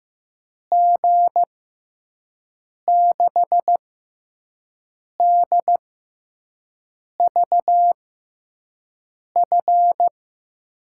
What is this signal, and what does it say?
Morse code, heard as a keyed tone, about 15 wpm, spaced as for 5 wpm, 700 Hz. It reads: G6DVF
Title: USA Call Signs - 6 Characters - 6x Character Spacing - 15wpm